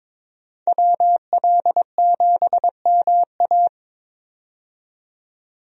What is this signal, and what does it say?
Morse code, heard as a keyed tone, about 22 wpm, 700 Hz. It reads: WL7MA
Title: USA Call Signs - 5 Characters - 22wpm